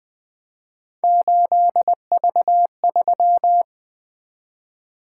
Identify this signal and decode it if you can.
Morse code, heard as a keyed tone, about 20 wpm, 700 Hz. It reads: 8V3
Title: Three Letter-Number Combinations - 20wpm